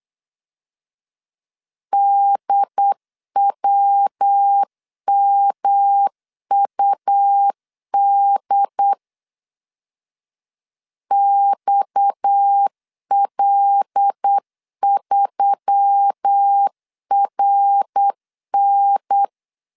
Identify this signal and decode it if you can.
Morse code, heard as a keyed tone, about 8 wpm, 800 Hz. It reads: DWMUD XL3RN
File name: 8wpm